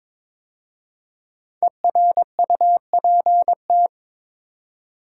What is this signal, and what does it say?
Morse code, heard as a keyed tone, about 22 wpm, 700 Hz. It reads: ERUPT
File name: 5 Letter Words - First 3 Letters Given - 22wpm